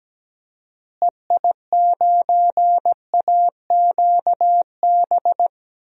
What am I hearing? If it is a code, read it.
Morse code, heard as a keyed tone, about 17 wpm, 700 Hz. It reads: EI9AQB
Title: DXCC-100 Call Signs - 6 Characters - Repeated 2x - 17wpm